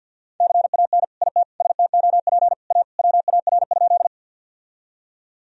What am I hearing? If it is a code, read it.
Morse code, heard as a keyed tone, about 50 wpm, 700 Hz. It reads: CAN IT STOP A WAR?
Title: Sentences from Top 1000 Words - Review Part 2 - 50wpm